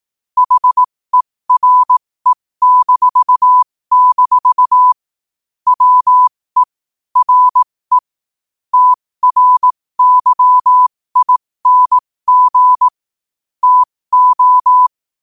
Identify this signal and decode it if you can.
Morse code, heard as a keyed tone, about 18 wpm, spaced as for 15 wpm, 1000 Hz. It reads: HERE-- WE RE TRYING TO
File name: qotdfiles / 15wpm